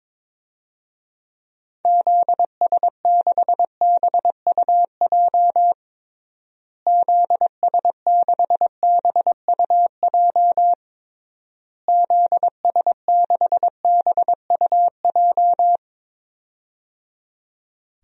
Morse code, heard as a keyed tone, about 22 wpm, 700 Hz. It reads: ZS6BUJ ZS6BUJ ZS6BUJ
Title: DXCC-100 Call Signs - 6 Characters - Repeated 3x - 22wpm